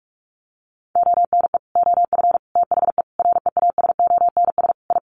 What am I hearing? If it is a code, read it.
Morse code, heard as a keyed tone, about 44 wpm, 700 Hz. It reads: ONE OF THE REASONS I
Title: Sets of 5 Words Encore2 45wpm